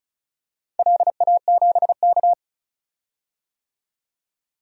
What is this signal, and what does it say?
Morse code, heard as a keyed tone, about 36 wpm, 690 Hz. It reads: LA7K
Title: DXCC-100 Call Signs - 4 Characters - 35wpm